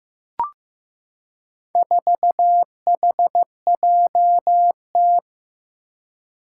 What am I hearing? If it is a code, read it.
Morse code, heard as a keyed tone, about 15 wpm, 700 Hz. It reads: 4HJT
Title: Four Challenging Letter-Number Combinations 15wpm